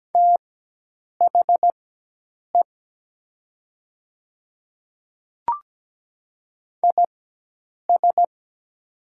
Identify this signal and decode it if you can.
Morse code, heard as a keyed tone, about 17 wpm, spaced as for 8 wpm, 700 Hz. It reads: THE IS
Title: 4x Character Spacing - Top 100 Words - 17wpm